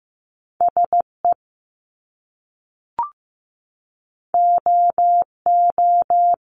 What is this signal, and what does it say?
Morse code, heard as a keyed tone, about 15 wpm, 700 Hz. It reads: SE OO
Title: Top Two Letter Combinations in Words 15wpm